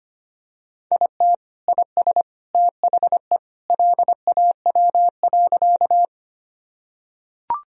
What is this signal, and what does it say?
Morse code, heard as a keyed tone, about 25 wpm, 700 Hz. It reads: IT IS THE LAW.
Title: Sentences from Top 1000 Words - Review Part 3 - 25wpm